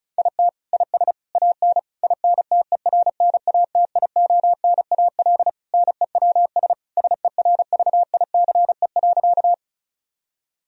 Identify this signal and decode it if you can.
Morse code, heard as a keyed tone, about 36 wpm, 700 Hz. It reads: IT IS AN INTERNATIONAL NEWS SERVICE.
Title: Sentences from Top 1000 Words - Review Part 4 - 35wpm